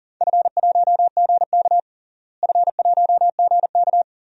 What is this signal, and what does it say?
Morse code, heard as a keyed tone, about 37 wpm, 700 Hz. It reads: F1GK F1GK
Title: DXCC-100 Call Signs - 4 Characters - Repeated 3x - 40wpm